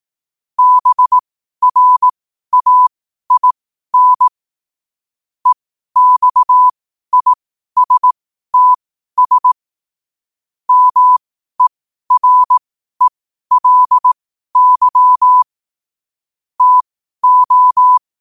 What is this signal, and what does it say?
Morse code, heard as a keyed tone, about 18 wpm, spaced as for 12 wpm, 1000 Hz. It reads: BRAIN EXISTS MERELY TO
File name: qotdfiles / 12wpm